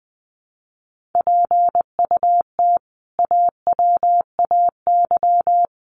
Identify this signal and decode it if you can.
Morse code, heard as a keyed tone, about 20 wpm, 700 Hz. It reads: PUT AWAY
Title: Sentences from Top 1400 Words 20wpm